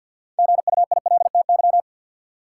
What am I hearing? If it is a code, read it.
Morse code, heard as a keyed tone, about 52 wpm, 700 Hz. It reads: GUILTY
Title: Six Letter Words - 50wpm